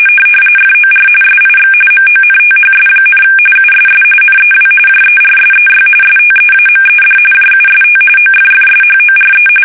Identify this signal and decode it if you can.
RTTY, about 45.45 baud. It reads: FOR SIX MONTHS, USS NEW JERSEY STEAMED JUST OFFSHORE
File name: fsk850 / battleshipnj